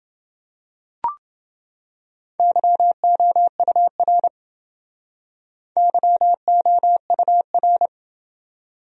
Morse code, heard as a keyed tone, about 29 wpm, 700 Hz. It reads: YOUR YOUR